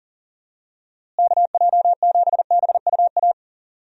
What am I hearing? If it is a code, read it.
Morse code, heard as a keyed tone, about 38 wpm, 700 Hz. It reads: KJ7BUA
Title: USA Call Signs - 6 Characters - Repeated 3x - 40wpm